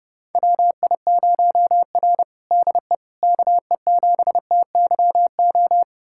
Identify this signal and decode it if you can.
Morse code, heard as a keyed tone, about 32 wpm, 700 Hz. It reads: WI0R DE KE7TYO
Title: USA Callsign DE Callsign 30wpm